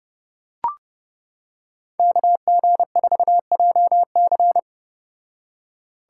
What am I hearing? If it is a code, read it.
Morse code, heard as a keyed tone, about 31 wpm, 700 Hz. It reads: KG4JC